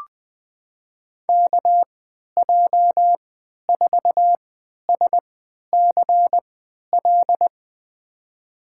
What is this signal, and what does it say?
Morse code, heard as a keyed tone, about 20 wpm, spaced as for 11 wpm, 700 Hz. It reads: KJ4SCL
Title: USA Call Signs - 6 Characters - 3x Character Spacing - 20wpm